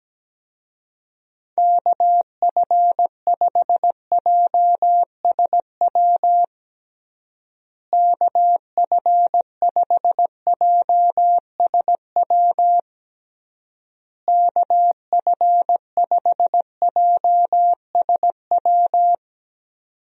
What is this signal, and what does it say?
Morse code, heard as a keyed tone, about 17 wpm, 700 Hz. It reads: KF5JSW KF5JSW KF5JSW
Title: USA Call Signs - 6 Characters - Repeated 3x - 17wpm